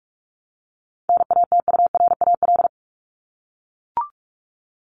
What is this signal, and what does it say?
Morse code, heard as a keyed tone, about 44 wpm, 700 Hz. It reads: NATURAL
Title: Top 501-750 Words 45wpm